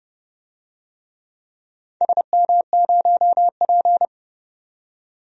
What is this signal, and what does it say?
Morse code, heard as a keyed tone, about 30 wpm, 700 Hz. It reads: SM0P
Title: DXCC-100 Call Signs - 4 Characters - 30wpm